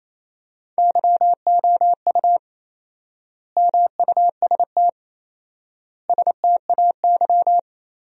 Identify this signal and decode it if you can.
Morse code, heard as a keyed tone, about 28 wpm, 700 Hz. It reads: YOU MUST STAY
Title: 4x Word-Spacing - Sentences from Top 500 Words - 28wpm